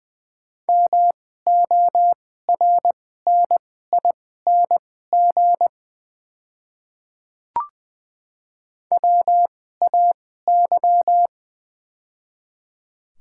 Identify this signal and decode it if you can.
Morse code, heard as a keyed tone, about 20 wpm, spaced as for 15 wpm, 700 Hz. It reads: MORNING WAY
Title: 2x Character Spacing - Top 500 Words - 20wpm